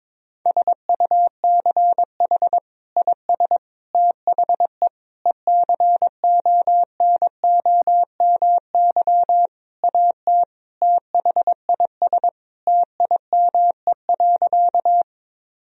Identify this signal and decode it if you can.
Morse code, heard as a keyed tone, about 21 wpm, 700 Hz. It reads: SUCH IS THE ECONOMY AT THIS TIME.
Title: Sentences from Top 1000 Words - Review Part 3 - 22wpm